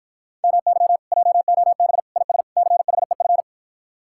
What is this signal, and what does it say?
Morse code, heard as a keyed tone, about 54 wpm, 700 Hz. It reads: MY JOB IS CHEF